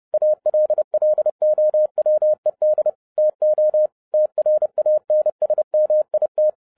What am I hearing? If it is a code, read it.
Morse code, heard as a keyed tone, about 30 wpm, 610 Hz. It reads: ALLOWED TO TRANSMIT
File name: 2011-07-mp3 / n40 EXCITING QRP CW OPERATION 30wpm